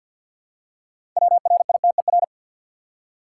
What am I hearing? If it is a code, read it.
Morse code, heard as a keyed tone, about 47 wpm, 700 Hz. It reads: WRITER
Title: Six Letter Words - 50wpm